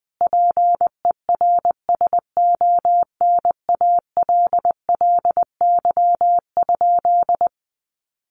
Morse code, heard as a keyed tone, about 20 wpm, 700 Hz. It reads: PERSONALLY?
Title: Sentences from Top 1400 Words 20wpm